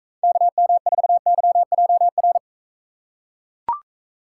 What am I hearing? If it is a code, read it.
Morse code, heard as a keyed tone, about 41 wpm, 700 Hz. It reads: KM4YJR